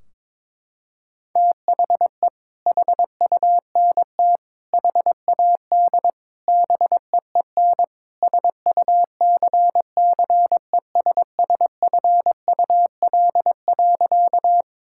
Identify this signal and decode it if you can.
Morse code, heard as a keyed tone, about 22 wpm, 700 Hz. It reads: THE HUNT HAD BEEN SUCCESSFUL.
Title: Sentences from Top 1600 Words 22wpm